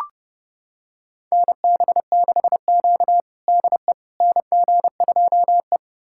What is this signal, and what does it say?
Morse code, heard as a keyed tone, about 30 wpm, 700 Hz. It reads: NB6Q DE NG2E
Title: USA Callsign DE Callsign - 4 Characters - 30wpm